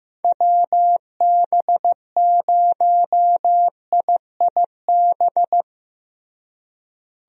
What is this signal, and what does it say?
Morse code, heard as a keyed tone, about 15 wpm, 700 Hz. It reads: WB0IIB
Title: USA Call Signs - 6 Characters - Repeated 3x - 15wpm